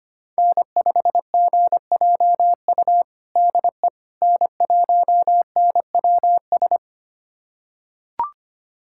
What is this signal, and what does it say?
Morse code, heard as a keyed tone, about 25 wpm, 700 Hz. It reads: N5GJU DE N1NWS
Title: USA Callsign DE Callsign - 5 Characters - 25wpm